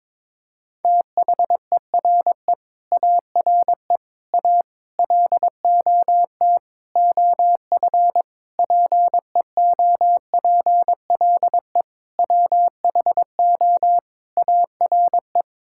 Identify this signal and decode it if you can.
Morse code, heard as a keyed tone, about 22 wpm, 700 Hz. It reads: THERE ARE A LOT OF PEOPLE WHO ARE
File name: practice / 22wpm / Sets of 8 Words - 22wpm